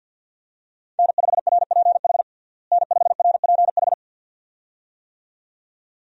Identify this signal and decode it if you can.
Morse code, heard as a keyed tone, about 51 wpm, 700 Hz. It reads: N5RPH N5RPH
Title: USA Call Signs - 5 Characters - Repeated 2x - 50wpm